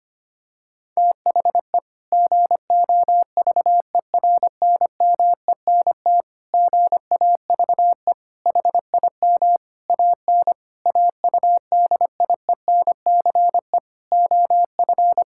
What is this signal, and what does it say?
Morse code, heard as a keyed tone, about 25 wpm, 700 Hz. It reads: THE GOVERNMENT GAVE HIM AN AUDIENCE OF
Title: Sentences from Top 1000 Words - Review Part 5 - 25wpm